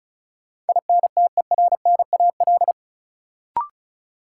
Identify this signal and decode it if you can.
Morse code, heard as a keyed tone, about 36 wpm, 700 Hz. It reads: INTERNAL
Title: Sentences from Top 1400 Words - New Word Drill Alt - 35wpm